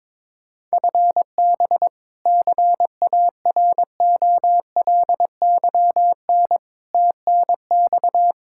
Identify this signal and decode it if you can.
Morse code, heard as a keyed tone, about 22 wpm, 700 Hz. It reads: FB CAROLYN TNX
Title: Practice QSO - Exchange 3 - More Info - 22wpm